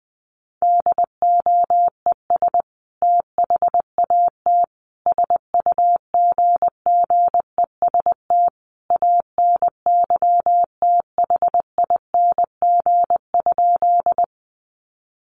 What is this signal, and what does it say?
Morse code, heard as a keyed tone, about 20 wpm, 700 Hz. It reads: DOES THAT SUGGEST ANYTHING?
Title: Sentences from Top 1300 Words 20wpm